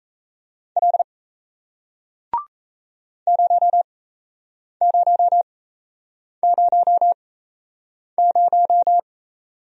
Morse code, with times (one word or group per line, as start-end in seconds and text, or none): 0.8-1.0 L
3.3-9.0 0000